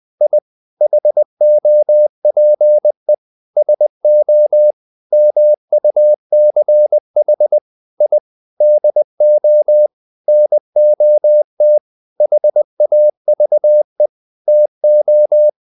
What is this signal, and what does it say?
Morse code, heard as a keyed tone, about 20 wpm, 600 Hz. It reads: I HOPE SO MUCH I DO NOT HAVE TO
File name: the old man and the sea part 2 20wpm